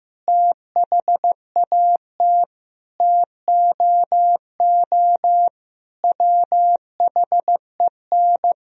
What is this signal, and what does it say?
Morse code, heard as a keyed tone, about 15 wpm, 700 Hz. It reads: THAT TOO WHEN
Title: Sentences from Top 1000 Words - Review Part 3 - 15wpm